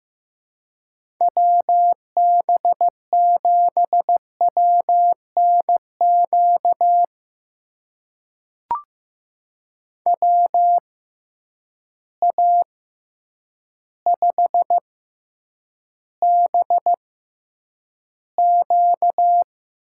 Morse code, heard as a keyed tone, about 15 wpm, spaced as for 8 wpm, 700 Hz. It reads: WB7WNQ WA5BQ
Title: USA Call Signs - 6 Characters - 6x Character Spacing - 15wpm